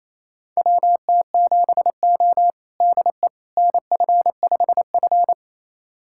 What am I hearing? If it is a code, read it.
Morse code, heard as a keyed tone, about 28 wpm, 700 Hz. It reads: WT7O DE NF5F
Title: USA Callsign DE Callsign - 4 Characters - 28wpm